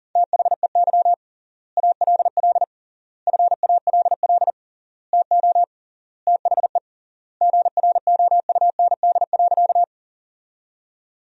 Morse code, heard as a keyed tone, about 40 wpm, 700 Hz. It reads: THEY ALL FALL TO THE GROUND.
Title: 3x Word-Spacing - Sentences from Top 500 Words - 40wpm